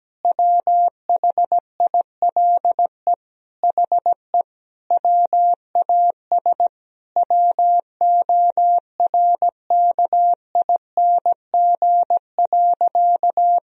Morse code, heard as a keyed tone, about 17 wpm, 700 Hz. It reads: WHILE HE WAS WORKING.
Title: Sentences from Top 1000 Words - Review Part 4 - 17wpm